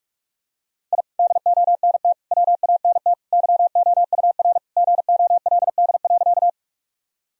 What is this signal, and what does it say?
Morse code, heard as a keyed tone, about 43 wpm, 700 Hz. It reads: I DONT WANT YOUR GOLD.
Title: Sentences from Top 1000 Words - Review Part 3 - 45wpm